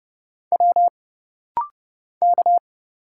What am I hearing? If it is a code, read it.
Morse code, heard as a keyed tone, about 30 wpm, 700 Hz. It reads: W K
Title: Single Letter-Number - Rapid-Fire 30wpm